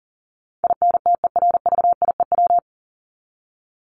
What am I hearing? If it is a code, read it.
Morse code, heard as a keyed tone, about 40 wpm, 700 Hz. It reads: INTERVIEW
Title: Top 1501-1750 Words 40wpm